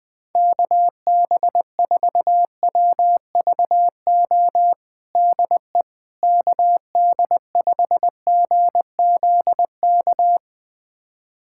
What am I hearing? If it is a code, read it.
Morse code, heard as a keyed tone, about 20 wpm, 700 Hz. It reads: KB4WVO DE KD5GZK
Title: USA Callsign DE Callsign 20wpm